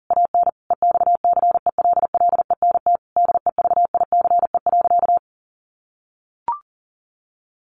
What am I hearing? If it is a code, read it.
Morse code, heard as a keyed tone, about 40 wpm, 700 Hz. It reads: AN EXCELLENT DEVICE.
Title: Sentences from Top 1500 Words 40wpm